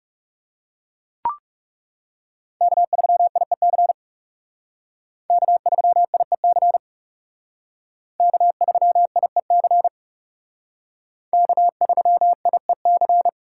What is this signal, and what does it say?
Morse code, heard as a keyed tone, about 37 wpm, 700 Hz. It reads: K3IEC K3IEC K3IEC K3IEC
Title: Speed Racing - USA 1x3 Call Signs 30wpm